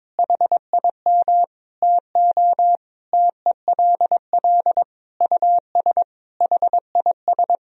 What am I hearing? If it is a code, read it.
Morse code, heard as a keyed tone, about 22 wpm, 700 Hz. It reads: HIM TO TELL US HIS
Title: Sentences from Top 1000 Words - Review Part 1 - 22wpm